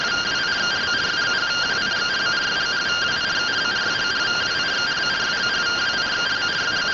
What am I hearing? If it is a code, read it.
Two-tone teleprinter signal, about 170 Hz shift.